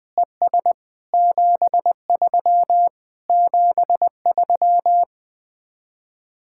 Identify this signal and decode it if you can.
Morse code, heard as a keyed tone, about 20 wpm, 700 Hz. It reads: ES 73 73
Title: Speed-Racing Common QSO Phrases - 20wpm